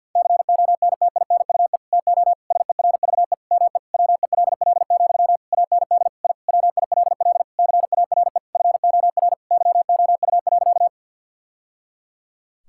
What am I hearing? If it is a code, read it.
Morse code, heard as a keyed tone, about 50 wpm, 700 Hz. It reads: CONTINUE TO SERVE ME WELL, AND I WILL CARE FOR YOU.
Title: Sentences from Top 1000 Words - Review Part 6 - 50wpm